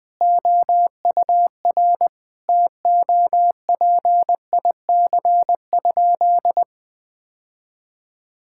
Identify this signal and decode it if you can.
Morse code, heard as a keyed tone, about 21 wpm, 700 Hz. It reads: OUR TOPIC?
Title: Sentences with 8-Word Phrases - 20wpm